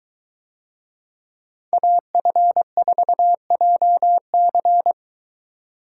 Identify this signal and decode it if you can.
Morse code, heard as a keyed tone, about 21 wpm, 700 Hz. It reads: AF4JC